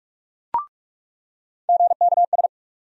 Morse code, heard as a keyed tone, about 46 wpm, 700 Hz. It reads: GKS
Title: USA Call Sign Postfixes - 3 Letters 45wpm